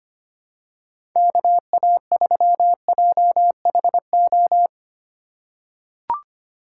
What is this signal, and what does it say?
Morse code, heard as a keyed tone, about 25 wpm, 700 Hz. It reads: KA3JHO
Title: USA Call Signs - 4x Character Spacing - 25wpm